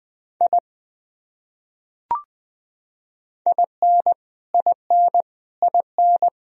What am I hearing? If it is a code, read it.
Morse code, heard as a keyed tone, about 20 wpm, 700 Hz. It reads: I IN IN IN
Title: Top 100 Words Repeated 20wpm